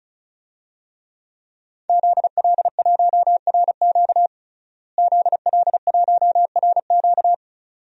Morse code, heard as a keyed tone, about 34 wpm, 700 Hz. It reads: ZL1RQ ZL1RQ
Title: DXCC-100 Call Signs - 5 Characters - Repeated 2x - 35wpm